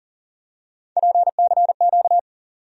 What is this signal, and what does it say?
Morse code, heard as a keyed tone, about 40 wpm, 700 Hz. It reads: PCQ
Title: USA Call Sign Postfixes - 3 Letters 40wpm